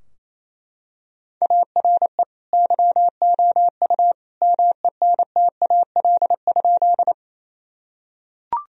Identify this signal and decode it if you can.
Morse code, heard as a keyed tone, about 27 wpm, 700 Hz. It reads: ARE YOU MENTAL?
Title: Sentences from Top 1000 Words - Review Part 4 - 28wpm